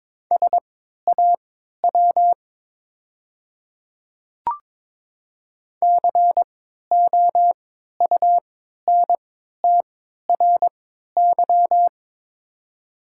Morse code, heard as a keyed tone, about 22 wpm, spaced as for 12 wpm, 700 Hz. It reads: SAW COUNTRY
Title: 3x Character Spacing - Top 300 Words - 22wpm